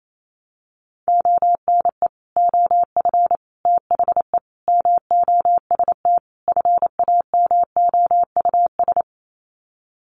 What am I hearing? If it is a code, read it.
Morse code, heard as a keyed tone, about 28 wpm, 700 Hz. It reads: ONE OF THE MOST FAMOUS
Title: Sets of 5 Words Encore2 28wpm